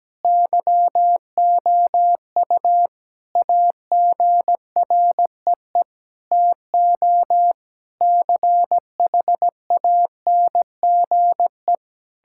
Morse code, heard as a keyed tone, about 17 wpm, 700 Hz. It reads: YOU AGREE TO CHANGE
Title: Sentences from Top 1000 Words - Review Part 6 - 17wpm